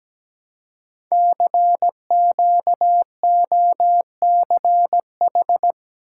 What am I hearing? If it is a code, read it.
Morse code, heard as a keyed tone, about 17 wpm, 700 Hz. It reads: CQOCH
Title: Code Groups - 5 Letter:Numbers - 17wpm